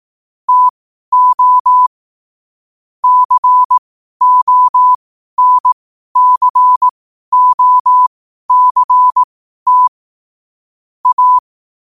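Morse code, heard as a keyed tone, about 18 wpm, spaced as for 12 wpm, 1000 Hz. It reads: TO CONCOCT A
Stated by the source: Morse code quotes at 12 words per minute.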